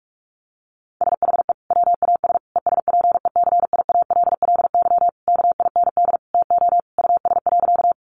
Morse code, heard as a keyed tone, about 44 wpm, 700 Hz. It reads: SHE WAS ESPECIALLY KIND TO US.
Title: Sentences from Top 700 Words 45wpm